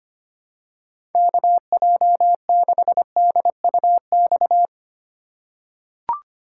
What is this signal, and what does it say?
Morse code, heard as a keyed tone, about 25 wpm, 700 Hz. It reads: KJ6DUX
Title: USA Call Signs - 6 Characters - 2x Character Spacing - 25wpm